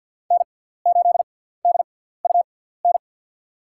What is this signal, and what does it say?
Morse code, heard as a keyed tone, about 50 wpm, spaced as for 17 wpm, 700 Hz. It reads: N8DUN
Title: USA Call Signs - 5 Characters - 6x Character Spacing - 50wpm